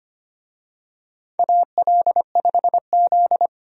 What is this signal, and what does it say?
Morse code, heard as a keyed tone, about 25 wpm, 700 Hz. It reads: AL5Z
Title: USA Call Signs - 4 Characters - 25wpm